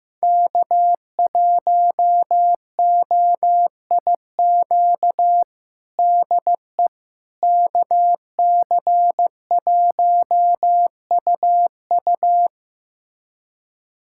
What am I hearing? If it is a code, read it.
Morse code, heard as a keyed tone, about 15 wpm, 700 Hz. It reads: K1OIQ DE KC1UU
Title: USA Callsign DE Callsign - 5 Characters - 15wpm